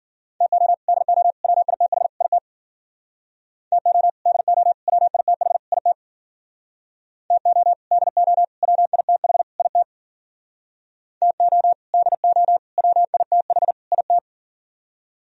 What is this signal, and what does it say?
Morse code, heard as a keyed tone, about 51 wpm, 700 Hz. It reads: TO DO WITH IT TO DO WITH IT TO DO WITH IT TO DO WITH IT